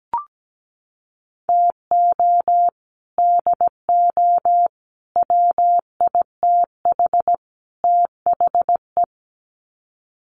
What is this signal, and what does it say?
Morse code, heard as a keyed tone, about 17 wpm, 700 Hz. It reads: TO DO WITH THE
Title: Sets of 4 Words 17wpm